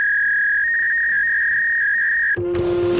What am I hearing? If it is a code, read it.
Fax machine handshake.